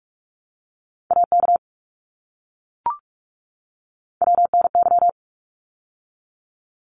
Morse code, heard as a keyed tone, about 44 wpm, 700 Hz. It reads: AK WNY
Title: ARRL Sections 45wpm